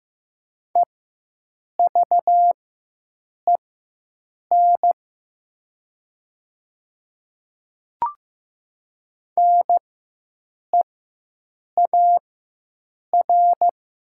Morse code, heard as a keyed tone, about 15 wpm, spaced as for 7 wpm, 700 Hz. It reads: EVEN NEAR
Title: 4x Character Spacing - Top 300 Words - 15wpm